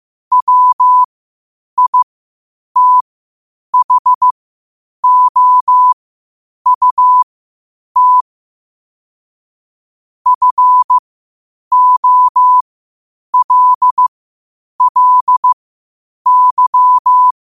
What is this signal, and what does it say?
Morse code, heard as a keyed tone, about 15 wpm, spaced as for 8 wpm, 1000 Hz. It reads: WITHOUT FOLLY
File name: qotdfiles / 8wpm